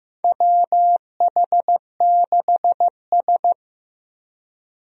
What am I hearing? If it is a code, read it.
Morse code, heard as a keyed tone, about 15 wpm, 700 Hz. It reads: WH6S
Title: USA Call Signs - 4 Characters - Repeated 2x - 15wpm